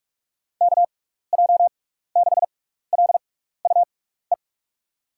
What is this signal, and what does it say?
Morse code, heard as a keyed tone, about 44 wpm, spaced as for 15 wpm, 700 Hz. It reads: KJ6LUE
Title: USA Call Signs - 6 Characters - 6x Character Spacing - 45wpm